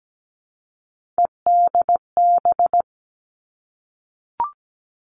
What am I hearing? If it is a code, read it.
Morse code, heard as a keyed tone, about 17 wpm, 700 Hz. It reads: EDB
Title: Three Challenging Letter-Number Combinations 17wpm